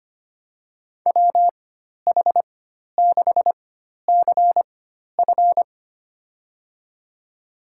Morse code, heard as a keyed tone, about 25 wpm, spaced as for 12 wpm, 700 Hz. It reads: WH6CF
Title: USA Call Signs - 4x Character Spacing - 25wpm